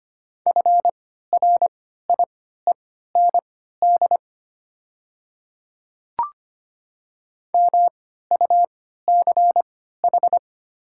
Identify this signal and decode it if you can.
Morse code, heard as a keyed tone, about 25 wpm, spaced as for 14 wpm, 700 Hz. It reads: FRIEND MUCH
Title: 3x Character Spacing - Top 500 Words - 25wpm